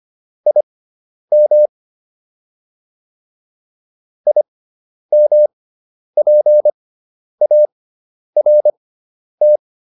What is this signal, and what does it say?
Morse code, heard as a keyed tone, about 25 wpm, spaced as for 10 wpm, 600 Hz. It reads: IM IMPART
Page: intermediate-practice-files